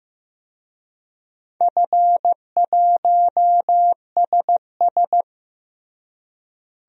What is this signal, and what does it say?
Morse code, heard as a keyed tone, about 15 wpm, 700 Hz. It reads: F1SS
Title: DXCC-100 Call Signs - 4 Characters - Repeated 2x - 15wpm